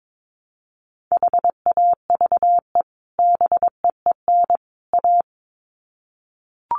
Morse code, heard as a keyed tone, about 22 wpm, 700 Hz. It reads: HAVE BEEN A
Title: Sets of 3 Words 22wpm